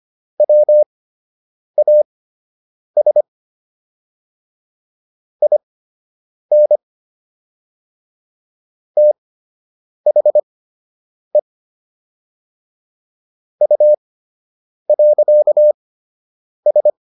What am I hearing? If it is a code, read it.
Morse code, heard as a keyed tone, about 25 wpm, spaced as for 8 wpm, 600 Hz. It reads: WAS IN THE U.S